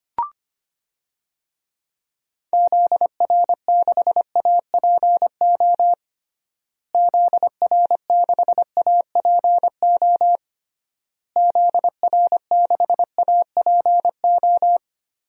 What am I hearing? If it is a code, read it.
Morse code, heard as a keyed tone, about 25 wpm, 700 Hz. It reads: ZR6APO ZR6APO ZR6APO
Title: DXCC-100 Call Signs - 6 Characters - Repeated 3x - 25wpm